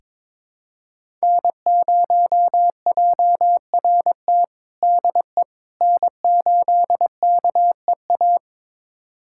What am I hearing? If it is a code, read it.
Morse code, heard as a keyed tone, about 21 wpm, 700 Hz. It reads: N0JRT DE N8KEA
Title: USA Callsign DE Callsign - 5 Characters - 22wpm